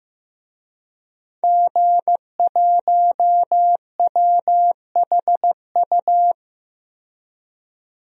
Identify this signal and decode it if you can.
Morse code, heard as a keyed tone, about 15 wpm, 700 Hz. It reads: G1WHU
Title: DXCC-100 Call Signs - 5 Characters - 15wpm